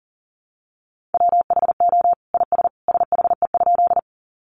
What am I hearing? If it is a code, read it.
Morse code, heard as a keyed tone, about 40 wpm, 700 Hz. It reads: WHO IS SHE?
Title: Sentences from Top 100 Words 40wpm